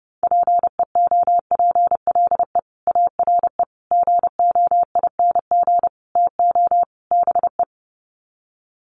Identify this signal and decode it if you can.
Morse code, heard as a keyed tone, about 28 wpm, 700 Hz. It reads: PEOPLE ARE GOING TO BE
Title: Sets of 5 Words Encore 30wpm